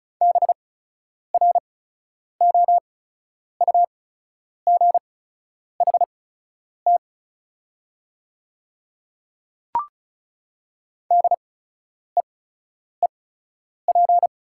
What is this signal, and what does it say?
Morse code, heard as a keyed tone, about 34 wpm, spaced as for 10 wpm, 690 Hz. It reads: BROUGHT DEEP
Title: 8x Character Spacing - Top 500 Words - 35wpm